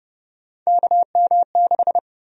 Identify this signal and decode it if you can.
Morse code, heard as a keyed tone, about 32 wpm, 700 Hz. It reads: KM6
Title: USA Call Sign Prefixes - 30wpm